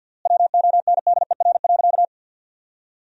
Morse code, heard as a keyed tone, about 50 wpm, 700 Hz. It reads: WONDER.
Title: 3x Word-Spacing - Sentences from Top 500 Words - 50wpm